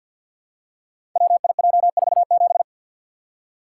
Morse code, heard as a keyed tone, about 50 wpm, 700 Hz. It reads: WIJ47
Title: Code Groups - 5 Letter:Numbers - 50wpm